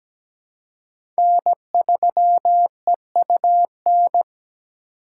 Morse code, heard as a keyed tone, about 17 wpm, 700 Hz. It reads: N3EUN
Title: USA Call Signs - 5 Characters - 4x Character Spacing - 17wpm